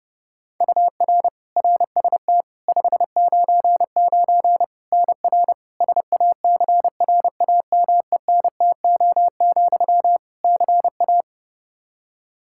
Morse code, heard as a keyed tone, about 30 wpm, 700 Hz. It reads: UR RST 599 NR SACRAMENTO, CA
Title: Practice QSO - Exchange 1 - Beginning - 30wpm